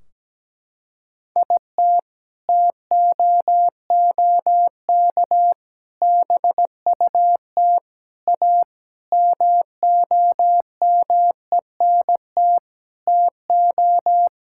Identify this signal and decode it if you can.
Morse code, heard as a keyed tone, about 17 wpm, 700 Hz. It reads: IT TOOK BUT A MOMENT TO
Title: Sentences from Top 1600 Words 17wpm